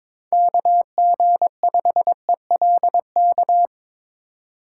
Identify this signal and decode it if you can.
Morse code, heard as a keyed tone, about 22 wpm, 700 Hz. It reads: KG5ELK
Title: USA Call Signs - 3x Character Spacing - 22wpm